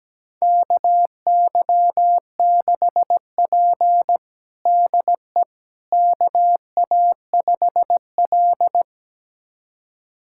Morse code, heard as a keyed tone, about 17 wpm, 700 Hz. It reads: KY6P DE KA5L
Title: USA Callsign DE Callsign - 4 Characters - 17wpm